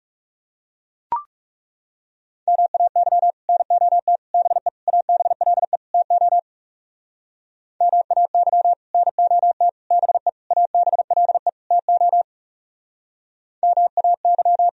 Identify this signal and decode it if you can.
Morse code, heard as a keyed tone, about 37 wpm, 700 Hz. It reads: MAY NOT BE ABLE TO MAY NOT BE ABLE TO MAY
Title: Speed-Racing - Sets of 5 Words 30wpm